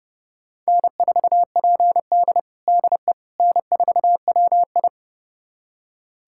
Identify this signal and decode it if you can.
Morse code, heard as a keyed tone, about 30 wpm, 700 Hz. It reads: N4PD DE N4WI
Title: USA Callsign DE Callsign - 4 Characters - 30wpm